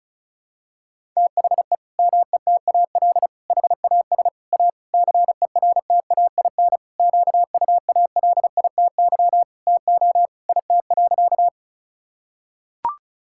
Morse code, heard as a keyed tone, about 36 wpm, 700 Hz. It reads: THE METAL HAS A CERTAIN QUALITY TO IT.
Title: Sentences from Top 1000 Words - Review Part 6 - 35wpm